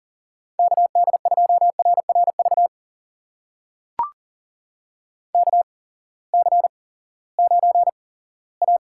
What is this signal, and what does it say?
Morse code, heard as a keyed tone, about 40 wpm, spaced as for 20 wpm, 700 Hz. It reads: KD2RRV KC9A
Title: USA Call Signs - 6 Characters - 8x Character Spacing - 40wpm